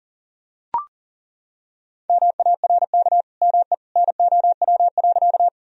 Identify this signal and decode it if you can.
Morse code, heard as a keyed tone, about 40 wpm, 700 Hz. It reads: MARK ME NOW.
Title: Sentences from Top 1000 Words - Review Part 1 - 40wpm